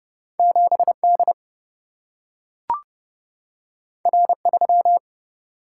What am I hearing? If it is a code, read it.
Morse code, heard as a keyed tone, about 29 wpm, 700 Hz. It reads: 7 D R 3
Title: Single-Letter Single-Number 30wpm